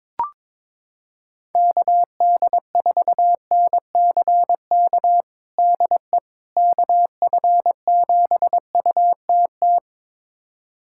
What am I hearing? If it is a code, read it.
Morse code, heard as a keyed tone, about 22 wpm, 700 Hz. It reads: KD4NCK DE KF7UTT
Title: USA Callsign DE Callsign 22wpm